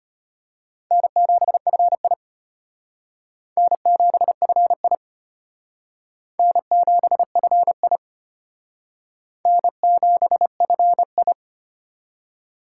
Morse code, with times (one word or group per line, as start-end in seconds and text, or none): 0.9-2.1 N7FI
3.6-5.0 N7FI
6.4-8.0 N7FI
9.5-11.3 N7FI